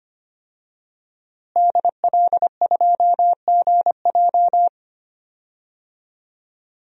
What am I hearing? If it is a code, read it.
Morse code, heard as a keyed tone, about 25 wpm, 700 Hz. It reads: DL2GJ
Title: DXCC-100 Call Signs - 5 Characters - Repeated 2x - 25wpm